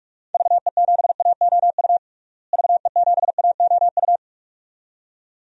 Morse code, heard as a keyed tone, about 46 wpm, 700 Hz. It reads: VE7AOU VE7AOU
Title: DXCC-100 Call Signs - 6 Characters - Repeated 2x - 45wpm